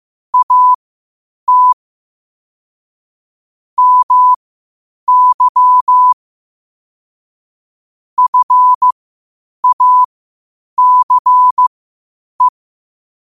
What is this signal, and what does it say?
Morse code, heard as a keyed tone, about 15 wpm, spaced as for 8 wpm, 1000 Hz. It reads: AT MY FACE
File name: qotdfiles / 8wpm